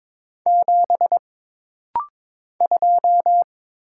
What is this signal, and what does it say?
Morse code, heard as a keyed tone, about 22 wpm, 700 Hz. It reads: 7 2
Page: This rapid-fire timing is geared to push your ICR (Instant Character Recognition) skills.